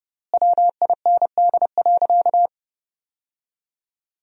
Morse code, heard as a keyed tone, about 32 wpm, 700 Hz. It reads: WIND.
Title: 4x Word-Spacing - Sentences from Top 400 Words - 30wpm